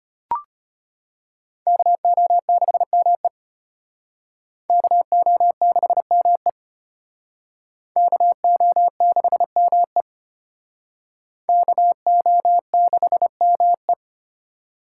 Morse code, heard as a keyed tone, about 31 wpm, 700 Hz. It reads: KO6ME KO6ME KO6ME KO6ME
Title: Speed Racing - USA 2x2 Call Signs 25wpm